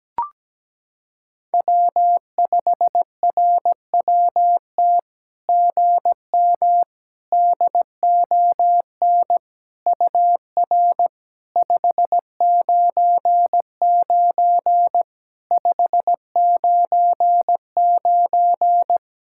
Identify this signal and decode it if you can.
Morse code, heard as a keyed tone, about 18 wpm, 700 Hz. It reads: W5RWT GM DON UR 599 599
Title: SOTA QSO Practice 17wpm